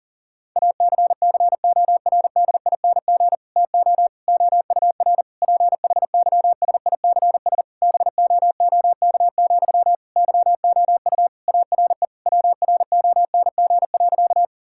Morse code, heard as a keyed tone, about 40 wpm, 700 Hz. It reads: ACCORDING TO OUR PHYSICS BOOK, YOU ARE WRONG.
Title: Sentences from Top 1000 Words - Review Part 5 - 40wpm